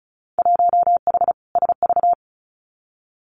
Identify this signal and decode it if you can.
Morse code, heard as a keyed tone, about 36 wpm, 700 Hz. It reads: 1H SV
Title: Winter Field Day Exchanges 35wpm